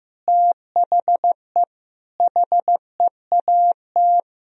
Morse code, heard as a keyed tone, about 16 wpm, 700 Hz. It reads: THE HEAT
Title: Sentences from Top 1000 Words - Review Part 3 - 15wpm